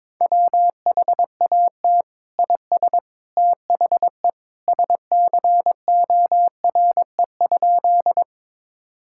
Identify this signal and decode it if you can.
Morse code, heard as a keyed tone, about 22 wpm, 700 Hz. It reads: WHAT IS THE SCORE?
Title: Sentences from Top 1600 Words 22wpm